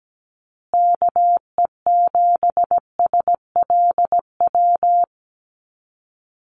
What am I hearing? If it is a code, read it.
Morse code, heard as a keyed tone, about 17 wpm, 700 Hz. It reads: KE7SLW
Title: USA Call Signs Repeated 2x 17wpm